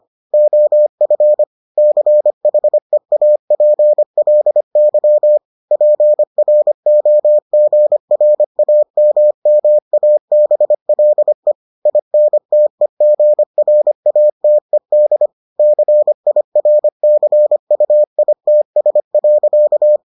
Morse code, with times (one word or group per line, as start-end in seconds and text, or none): 0.3-1.4 OF
1.8-5.4 CHEAPLY
5.7-11.5 PROGRAMMABLE
11.9-15.3 INTEGRATED
15.6-20.0 CIRCUITS.